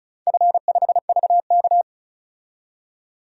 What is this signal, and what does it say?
Morse code, heard as a keyed tone, about 36 wpm, 700 Hz. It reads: F5VK
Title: DXCC-100 Call Signs - 4 Characters - 35wpm